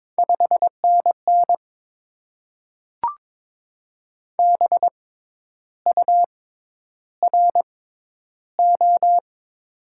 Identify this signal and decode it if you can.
Morse code, heard as a keyed tone, about 22 wpm, spaced as for 10 wpm, 700 Hz. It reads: 5NN BURO
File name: Top 90 QSO Elements - 6x Character Spacing - 22wpm